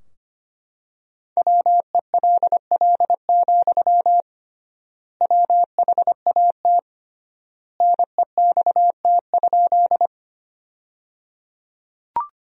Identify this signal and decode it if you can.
Morse code, heard as a keyed tone, about 25 wpm, 700 Hz. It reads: WELL, WHAT NEXT?
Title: 3x Word-Spacing - Sentences from Top 300 Words - 25wpm